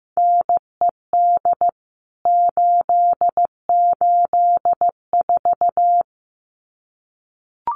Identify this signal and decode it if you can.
Morse code, heard as a keyed tone, about 14 wpm, 700 Hz. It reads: NED 884
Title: CWT Exchange 15wpm